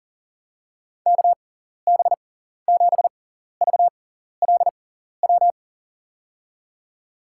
Morse code, heard as a keyed tone, about 37 wpm, spaced as for 14 wpm, 700 Hz. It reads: KB7VLW
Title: USA Call Signs - 6 Characters - 6x Character Spacing - 40wpm